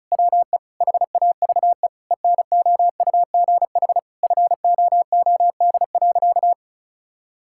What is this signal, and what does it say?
Morse code, heard as a keyed tone, about 36 wpm, 700 Hz. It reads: WE HAVE ENOUGH FOOD.
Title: Sentences from Top 1000 Words - Review Part 5 - 35wpm